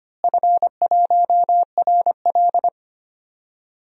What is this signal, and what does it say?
Morse code, heard as a keyed tone, about 25 wpm, 700 Hz. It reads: F1RL
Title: DXCC-100 Call Signs - 4 Characters - Repeated 2x - 25wpm